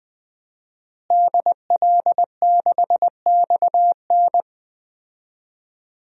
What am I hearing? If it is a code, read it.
Morse code, heard as a keyed tone, about 20 wpm, 700 Hz. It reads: DL6XN
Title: DXCC-100 Call Signs - 5 Characters - Repeated 3x - 20wpm